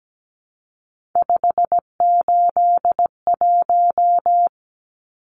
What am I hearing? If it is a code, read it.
Morse code, heard as a keyed tone, about 17 wpm, 700 Hz. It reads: 581
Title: Numbers - 3-Digit 17wpm